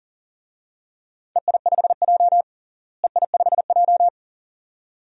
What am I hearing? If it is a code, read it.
Morse code, heard as a keyed tone, about 40 wpm, 700 Hz. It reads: EI5J EI5J
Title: DXCC-100 Call Signs - 4 Characters - Repeated 2x - 40wpm